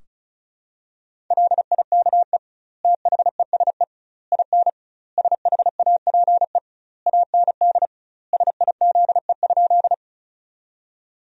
Morse code, heard as a keyed tone, about 36 wpm, 700 Hz. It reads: LIKE THESE IN SHAPE AND SIZE?
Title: 2x Word-Spacing - Sentences from Top 500 Words - 35wpm